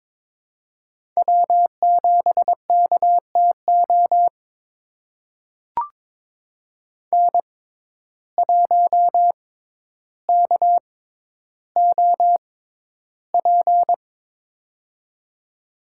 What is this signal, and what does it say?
Morse code, heard as a keyed tone, about 22 wpm, spaced as for 11 wpm, 700 Hz. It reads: W7KTO N1KOP
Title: USA Call Signs - 5 Characters - 6x Character Spacing - 22wpm